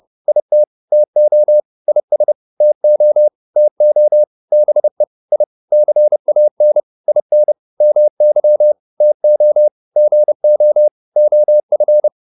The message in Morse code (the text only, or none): IT TO IS TO TO BE I CAN IN MY TO GO OF